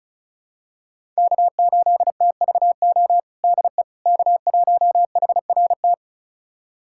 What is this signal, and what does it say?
Morse code, heard as a keyed tone, about 34 wpm, 700 Hz. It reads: K8TVO DE K1HRT
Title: USA Callsign DE Callsign - 5 Characters - 35wpm